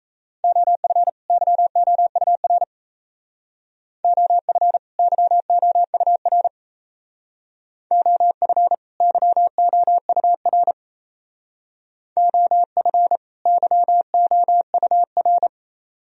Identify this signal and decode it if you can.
Morse code, heard as a keyed tone, about 35 wpm, 700 Hz. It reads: OF YOUR OF YOUR OF YOUR OF YOUR
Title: Speed-Racing - Sets of 2 Words 28wpm